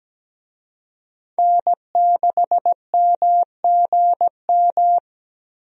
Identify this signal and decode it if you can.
Morse code, heard as a keyed tone, about 17 wpm, 700 Hz. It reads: N6MGM
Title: USA Call Signs - 5 Characters - Repeated 3x - 17wpm